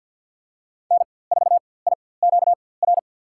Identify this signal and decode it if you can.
Morse code, heard as a keyed tone, about 50 wpm, spaced as for 23 wpm, 700 Hz. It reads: N4IQR
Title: USA Call Signs - 4x Character Spacing - 50wpm